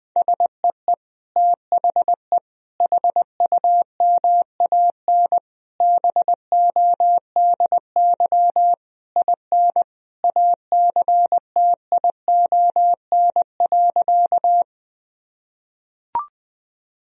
Morse code, heard as a keyed tone, about 20 wpm, 700 Hz. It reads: SEE THE HUMAN BODY IN ACTION.
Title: Sentences from Top 1000 Words - Review Part 4 - 20wpm